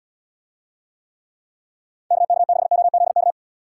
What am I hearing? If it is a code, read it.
Morse code, heard as a keyed tone, about 78 wpm, 700 Hz. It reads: KK6OZK